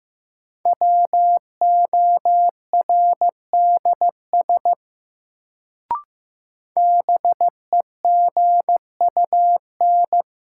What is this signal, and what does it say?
Morse code, heard as a keyed tone, about 15 wpm, 700 Hz. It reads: WORDS BEGUN
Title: Sentences from Top 1000 Words - Review Part 6 - New Word Drill - 15wpm